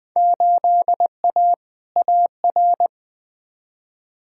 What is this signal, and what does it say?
Morse code, heard as a keyed tone, about 20 wpm, 700 Hz. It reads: 8A AR
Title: Field Day Exchanges Encore - 20wpm